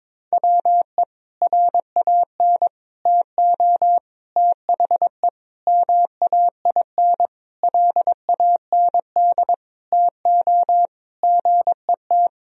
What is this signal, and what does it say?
Morse code, heard as a keyed tone, about 21 wpm, 700 Hz. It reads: WE RAN TO THE MAIN LAND TO GET
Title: Sentences from Top 1000 Words - Review Part 2 - 22wpm